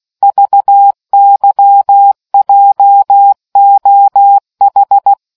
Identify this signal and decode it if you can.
Morse code, heard as a keyed tone, about 16 wpm, 800 Hz. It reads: VYJOH